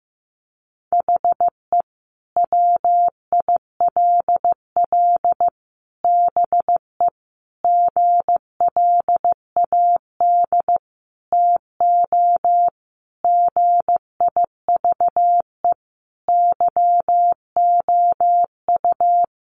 Morse code, heard as a keyed tone, about 15 wpm, 700 Hz. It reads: HE WILL BE GLAD TO GIVE YOU
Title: Sentences from Top 800 Words 15wpm